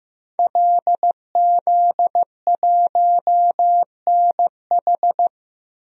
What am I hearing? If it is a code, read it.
Morse code, heard as a keyed tone, about 15 wpm, 700 Hz. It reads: LZ1NH
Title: DXCC-100 Call Signs - 5 Characters - Repeated 2x - 15wpm